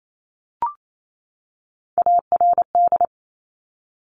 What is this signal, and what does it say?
Morse code, heard as a keyed tone, about 28 wpm, 700 Hz. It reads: ARD
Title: Top Three Letter Combinations in Words 28wpm